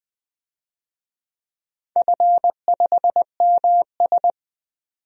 Morse code, heard as a keyed tone, about 20 wpm, 700 Hz. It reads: F5MS
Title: DXCC-100 Call Signs - 4 Characters - 20wpm